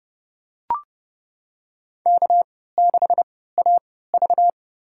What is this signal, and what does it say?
Morse code, heard as a keyed tone, about 32 wpm, spaced as for 17 wpm, 700 Hz. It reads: K6AV
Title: USA Call Signs - 4 Characters - 3x Character Spacing - 30wpm